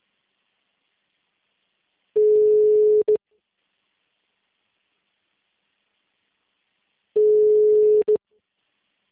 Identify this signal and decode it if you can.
Ringback tone (Europe, 425 Hz), with speech.